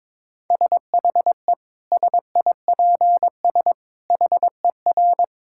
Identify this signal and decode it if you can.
Morse code, heard as a keyed tone, about 22 wpm, 700 Hz. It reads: SHE SIPS HER
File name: Difficult Sentences with Hints - 22wpm